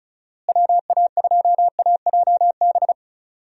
Morse code, heard as a keyed tone, about 34 wpm, 700 Hz. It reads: WA2AJB
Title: USA Call Signs - 6 Characters - Repeated 3x - 35wpm